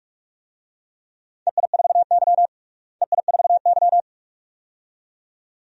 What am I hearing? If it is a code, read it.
Morse code, heard as a keyed tone, about 46 wpm, 700 Hz. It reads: EI4Y EI4Y
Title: DXCC-100 Call Signs - 4 Characters - Repeated 2x - 45wpm